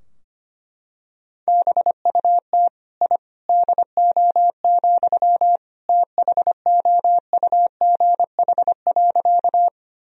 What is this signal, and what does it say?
Morse code, heard as a keyed tone, about 25 wpm, 700 Hz. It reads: BUT I DO, THOUGH.
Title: Sentences from Top 1000 Words - Review Part 2 - 25wpm